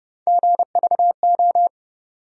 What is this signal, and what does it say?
Morse code, heard as a keyed tone, about 30 wpm, 700 Hz. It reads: GVO
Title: USA Call Sign Postfixes - 3 Letters 30wpm